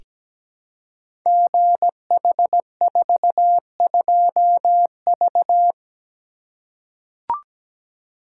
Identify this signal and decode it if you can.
Morse code, heard as a keyed tone, about 17 wpm, 700 Hz. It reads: GH42V
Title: Code Groups - 5 Letter:Numbers - 17wpm